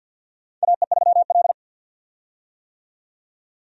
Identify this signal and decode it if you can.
Morse code, heard as a keyed tone, about 53 wpm, 700 Hz. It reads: AE3L
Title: USA Call Signs - 4 Characters - 50wpm